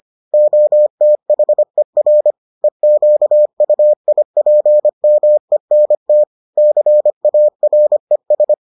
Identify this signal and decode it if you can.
Morse code, heard as a keyed tone, about 25 wpm, 600 Hz. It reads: OTHER EQUIPMENT CARES